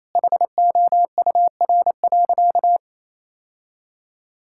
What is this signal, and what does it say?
Morse code, heard as a keyed tone, about 28 wpm, 700 Hz. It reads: HOUR.
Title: Sentences from Top 1000 Words - Review Part 5 - 28wpm